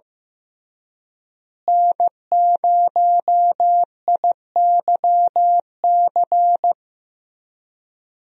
Morse code, heard as a keyed tone, about 15 wpm, 700 Hz. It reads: N0IYC
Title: USA Call Signs - 5 Characters - Repeated 3x - 15wpm